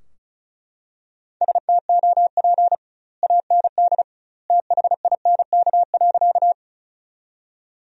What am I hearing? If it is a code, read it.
Morse code, heard as a keyed tone, about 36 wpm, 700 Hz. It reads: STOP AND THINK.
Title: 2x Word-Spacing - Sentences from Top 300 Words - 35wpm